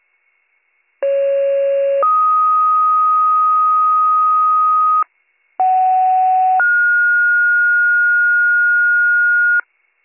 Two-tone page, heard as a one-second tone, then a three-second tone, about 550 then 1250 Hz, about 750 then 1430 Hz, with noise. It is heard from 1.0 to 9.6 s.